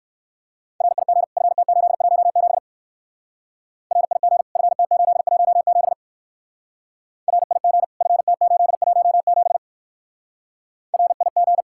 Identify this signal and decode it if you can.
Morse code, heard as a keyed tone, about 58 wpm, 700 Hz. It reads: RIG FT817 RIG FT817 RIG FT817 RIG